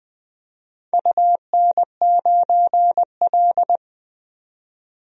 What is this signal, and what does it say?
Morse code, heard as a keyed tone, about 20 wpm, 700 Hz. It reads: UN9L
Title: DXCC-100 Call Signs - 4 Characters - 20wpm